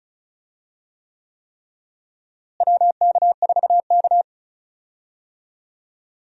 Morse code, heard as a keyed tone, about 36 wpm, 700 Hz. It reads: WK4K
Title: USA Call Signs - 4 Characters - Repeated 3x - 35wpm